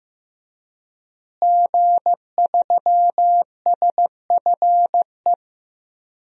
Morse code, heard as a keyed tone, about 15 wpm, 700 Hz. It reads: G3SFE
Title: DXCC-100 Call Signs - 5 Characters - Repeated 2x - 15wpm